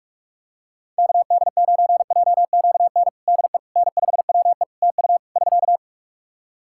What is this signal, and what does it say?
Morse code, heard as a keyed tone, about 41 wpm, 700 Hz. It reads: KD9JQN DE N5WE TU <SK>
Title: Practice QSO - Exchange 4 - Ending - 45wpm